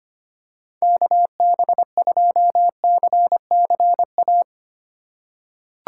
Morse code, heard as a keyed tone, about 25 wpm, 700 Hz. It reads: KB2CCA
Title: USA Call Signs - 6 Characters - 4x Character Spacing - 25wpm